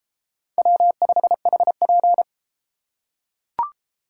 Morse code, heard as a keyed tone, about 31 wpm, 700 Hz. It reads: W5HP